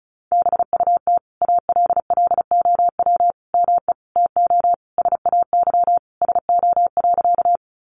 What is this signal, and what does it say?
Morse code, heard as a keyed tone, about 34 wpm, 700 Hz. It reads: BUT ALLOW ME TO SAY SO.
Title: Sentences from Top 700 Words 35wpm